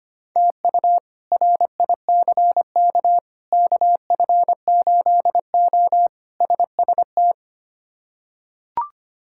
Morse code, heard as a keyed tone, about 25 wpm, 700 Hz. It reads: TU RICK KF8O SST
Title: K1USN SST QSO Practice - 25wpm